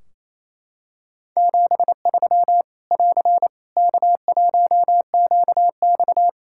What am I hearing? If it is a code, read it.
Morse code, heard as a keyed tone, about 27 wpm, 700 Hz. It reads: 73 <AR> K1QX
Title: K1USN SST QSO Practice 28wpm